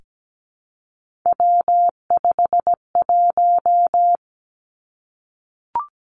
Morse code, heard as a keyed tone, about 17 wpm, 700 Hz. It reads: W51
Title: Three Challenging Letter-Number Combinations 17wpm